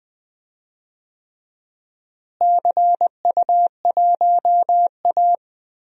Morse code, heard as a keyed tone, about 20 wpm, 700 Hz. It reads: CU1A
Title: DXCC-100 Call Signs - 4 Characters - Repeated 3x - 20wpm